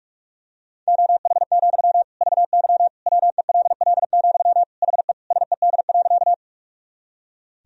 Morse code, heard as a keyed tone, about 42 wpm, 690 Hz. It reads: OH, VY WELL, HE SED.
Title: Sentences from Top 500 Words and Abbreviations 45wpm